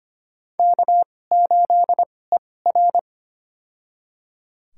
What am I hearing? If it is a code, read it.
Morse code, heard as a keyed tone, about 25 wpm, spaced as for 18 wpm, 700 Hz. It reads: K8ER
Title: USA Call Signs - 4 Characters - 2x Character Spacing - 25wpm